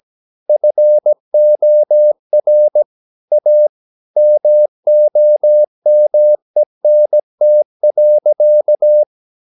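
Morse code, heard as a keyed tone, about 17 wpm, 600 Hz. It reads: FOR A MOMENT.
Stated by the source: 17 wpm Morse practice sending the English text of a novel.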